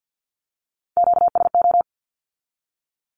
Morse code, heard as a keyed tone, about 50 wpm, 700 Hz. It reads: QSO
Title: Q-Codes with Spoken Meaning - 50wpm